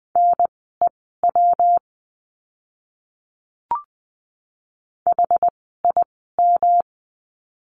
Morse code, heard as a keyed tone, about 20 wpm, spaced as for 15 wpm, 700 Hz. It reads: NEW HIM
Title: 146 Lesson 24f - Letter y - Words 20wpm